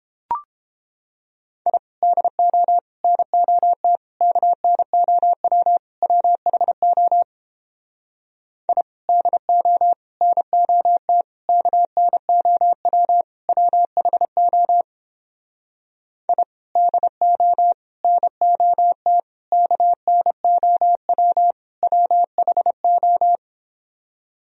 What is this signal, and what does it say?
Morse code, heard as a keyed tone, about 29 wpm, 700 Hz. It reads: I DO NOT KNOW WHO I DO NOT KNOW WHO I DO NOT KNOW WHO